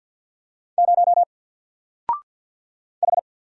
Morse code, heard as a keyed tone, about 48 wpm, 700 Hz. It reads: 0 H
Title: Single Letter-Number 50wpm